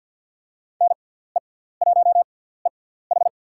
Morse code, heard as a keyed tone, about 47 wpm, spaced as for 17 wpm, 700 Hz. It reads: NE1EH
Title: USA Call Signs - 5 Characters - 6x Character Spacing - 50wpm